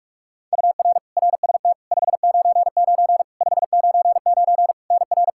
Morse code, heard as a keyed tone, about 42 wpm, 700 Hz. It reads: UR RST 599 599 NR
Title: Practice QSO - Exchange 1 - Beginning - 45wpm